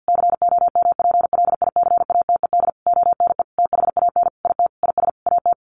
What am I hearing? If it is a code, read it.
Morse code, heard as a keyed tone, about 50 wpm, 700 Hz. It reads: COMPLICATED ONE THAN IT IS AT
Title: Sentences from Top 1200 Words 50wpm